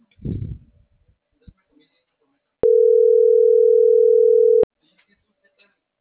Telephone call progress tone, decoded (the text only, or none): ringback tone